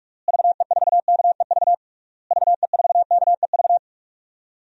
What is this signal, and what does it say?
Morse code, heard as a keyed tone, about 45 wpm, 700 Hz. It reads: VE4KEV VE4KEV
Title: DXCC-100 Call Signs - 6 Characters - Repeated 3x - 45wpm